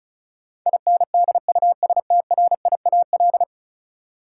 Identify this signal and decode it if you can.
Morse code, heard as a keyed tone, about 36 wpm, 700 Hz. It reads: INDUSTRIAL
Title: Sets of 5 Words - New Word Drill - 35wpm